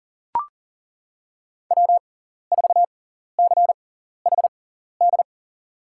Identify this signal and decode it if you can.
Morse code, heard as a keyed tone, about 40 wpm, spaced as for 14 wpm, 700 Hz. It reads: W4CHD
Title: USA Call Signs - 5 Characters - 6x Character Spacing - 40wpm